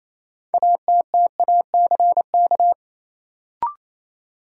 Morse code, heard as a keyed tone, about 28 wpm, 700 Hz. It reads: ATTACK
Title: Sentences from Top 1500 Words - New Word Drill - 28wpm